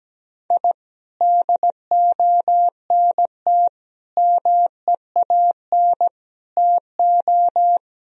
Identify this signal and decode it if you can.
Morse code, heard as a keyed tone, about 17 wpm, 700 Hz. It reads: I DONT MEAN TO
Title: Sentences from Top 1000 Words - Review Part 1 - 17wpm